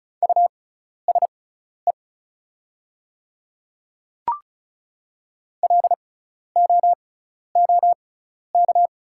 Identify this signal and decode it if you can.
Morse code, heard as a keyed tone, about 36 wpm, spaced as for 12 wpm, 700 Hz. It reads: USE LOOK
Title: 6x Character Spacing - Top 100 Words - 35wpm